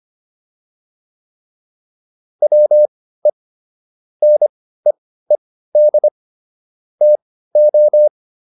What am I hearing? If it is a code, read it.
Morse code, heard as a keyed tone, about 25 wpm, spaced as for 15 wpm, 600 Hz. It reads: WE NEED TO
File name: WD404 15